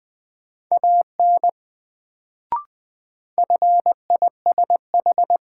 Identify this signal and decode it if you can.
Morse code, heard as a keyed tone, about 21 wpm, 700 Hz. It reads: AN FISH
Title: Sentences from Top 300 Words - New Word Drill - 20wpm